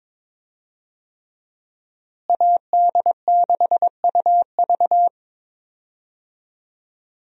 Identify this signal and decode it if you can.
Morse code, heard as a keyed tone, about 22 wpm, 700 Hz. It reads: AD6UV
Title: USA Call Signs - 5 Characters - Repeated 3x - 22wpm